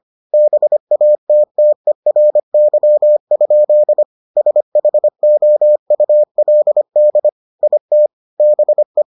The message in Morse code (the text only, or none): BATTERY? SHOULD IT BE